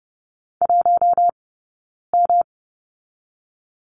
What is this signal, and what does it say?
Morse code, heard as a keyed tone, about 30 wpm, 700 Hz. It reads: 1 M
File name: Lesson 19d - Letter w - Two Characters 30wpm